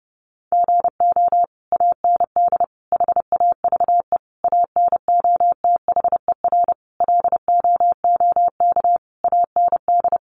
Morse code, heard as a keyed tone, about 30 wpm, 700 Hz. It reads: GO AND HAVE ANOTHER LOOK AND
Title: Sentences from Top 600 Words 30wpm